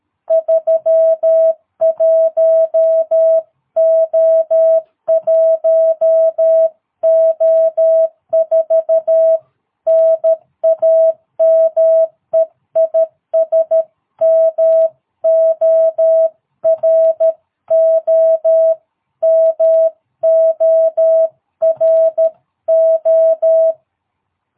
Morse code, heard as a keyed tone, about 13 wpm, 650 Hz. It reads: 31O1O4 NAME IS MORO MORO
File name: jh8cbh / 2022cw